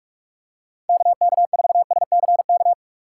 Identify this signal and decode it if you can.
Morse code, heard as a keyed tone, about 46 wpm, 700 Hz. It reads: KK4SCK
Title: USA Call Signs - 3x Character Spacing - 45wpm